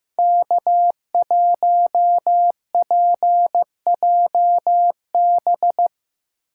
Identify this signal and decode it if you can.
Morse code, heard as a keyed tone, about 16 wpm, 700 Hz. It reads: K1PJB
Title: Speed Racing - USA 1x3 Call Signs 15wpm